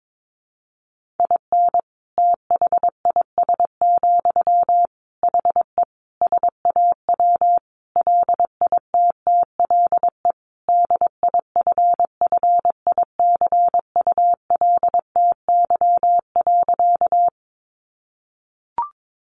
Morse code, heard as a keyed tone, about 22 wpm, 700 Hz. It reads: IN THIS, HE SAW LITTLE DIFFICULTY.
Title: Sentences from Top 1500 Words 22wpm